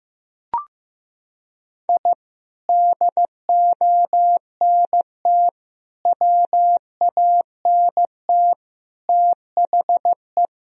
Morse code, heard as a keyed tone, about 15 wpm, 700 Hz. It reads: I DONT WANT THE
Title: Sentences from Top 1000 Words - Review Part 2 - 15wpm